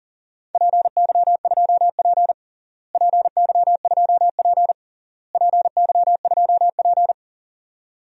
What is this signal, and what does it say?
Morse code, heard as a keyed tone, about 40 wpm, 700 Hz. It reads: PY2P PY2P PY2P
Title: DXCC-100 Call Signs - 4 Characters - Repeated 3x - 40wpm